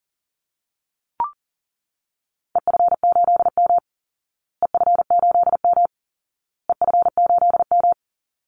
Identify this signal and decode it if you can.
Morse code, heard as a keyed tone, about 40 wpm, 700 Hz. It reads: EF8M EF8M EF8M
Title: Call Signs Repeated 2x - 40wpm